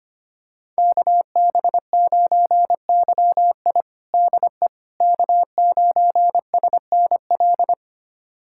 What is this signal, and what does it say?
Morse code, heard as a keyed tone, about 25 wpm, 700 Hz. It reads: KB9YI DE K9SNL
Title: USA Callsign DE Callsign - 5 Characters - 25wpm